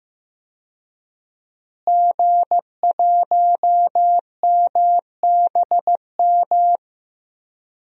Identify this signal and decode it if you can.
Morse code, heard as a keyed tone, about 16 wpm, 700 Hz. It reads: G1MBM
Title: DXCC-100 Call Signs - 5 Characters - Repeated 2x - 15wpm